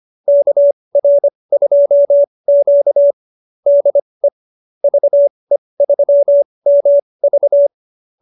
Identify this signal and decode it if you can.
Morse code, heard as a keyed tone, about 25 wpm, spaced as for 20 wpm, 590 Hz. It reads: KR2Q DE VE3MV